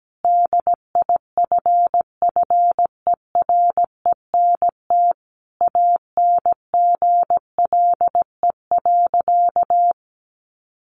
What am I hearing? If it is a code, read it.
Morse code, heard as a keyed tone, about 17 wpm, 700 Hz. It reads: DIFFERENT ANGLE.
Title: Sentences from Top 1500 Words 17wpm